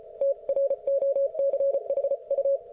Morse code, heard as a keyed tone, about 34 wpm, 550 Hz. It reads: TROCHU